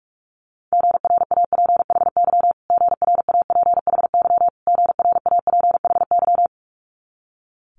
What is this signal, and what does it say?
Morse code, heard as a keyed tone, about 46 wpm, 700 Hz. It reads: GRAPHY GRAPHY GRAPHY
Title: Multi-letter Phonemes Repeated 3x 45wpm